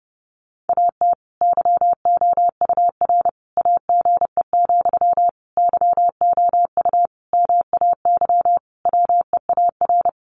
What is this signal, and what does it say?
Morse code, heard as a keyed tone, about 30 wpm, 700 Hz. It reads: AT YOUR AGE, YOU MAY WEAR
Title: Sentences from Top 1500 Words 30wpm